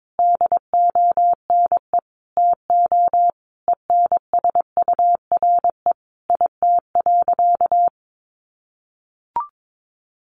Morse code, heard as a keyed tone, about 22 wpm, 700 Hz. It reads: DONE TO ENSURE IT.
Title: Sentences from Top 1100 Words 22wpm